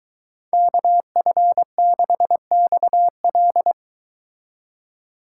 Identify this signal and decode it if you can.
Morse code, heard as a keyed tone, about 23 wpm, 700 Hz. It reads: KF6XL